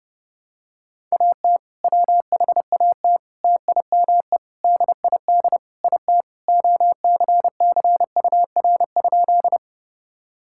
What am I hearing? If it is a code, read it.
Morse code, heard as a keyed tone, about 30 wpm, 700 Hz. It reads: AT WHAT TIME DID IT OCCUR?
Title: Sentences from Top 1000 Words - Review Part 6 - 30wpm